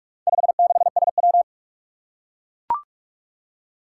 Morse code, heard as a keyed tone, about 44 wpm, 700 Hz. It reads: 56SW
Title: Four Challenging Letter-Number Combinations 45wpm